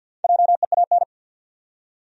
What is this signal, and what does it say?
Morse code, heard as a keyed tone, about 53 wpm, 700 Hz. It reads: JEAN
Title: USA Common Names - 50wpm